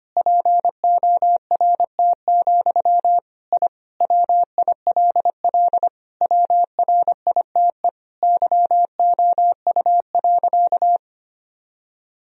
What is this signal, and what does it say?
Morse code, heard as a keyed tone, about 25 wpm, 700 Hz. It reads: PORT, I WILL WRITE YOU.
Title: Sentences from Top 1000 Words - Review Part 1 - 25wpm